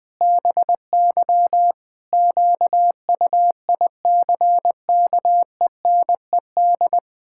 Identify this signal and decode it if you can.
Morse code, heard as a keyed tone, about 20 wpm, 700 Hz. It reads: BY QUICKENED
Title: Pangram Challenge v2 - 20wpm